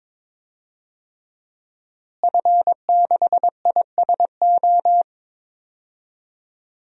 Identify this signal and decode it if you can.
Morse code, heard as a keyed tone, about 22 wpm, 700 Hz. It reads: F6ISO
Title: DXCC-100 Call Signs - 5 Characters - Repeated 3x - 22wpm